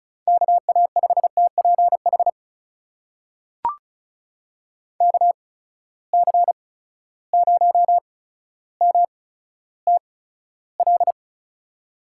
Morse code, heard as a keyed tone, about 36 wpm, spaced as for 15 wpm, 700 Hz. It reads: KA5TPH KC0MTL
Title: USA Call Signs - 6 Characters - 8x Character Spacing - 35wpm